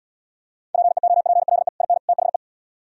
Morse code, heard as a keyed tone, about 72 wpm, 700 Hz. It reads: PEOPLE IN THE